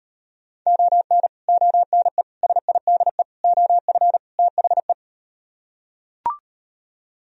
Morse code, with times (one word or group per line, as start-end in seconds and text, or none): 0.7-1.3 ON
1.5-2.2 ONE
2.4-3.2 SIDE
3.4-4.2 OF
4.4-4.9 THE